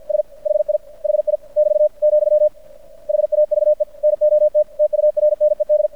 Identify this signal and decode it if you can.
Morse code, heard as a keyed tone, about 55 wpm, 600 Hz. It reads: S HI HI <BT> , HAVE NOT TURNED
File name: archived / CRSnet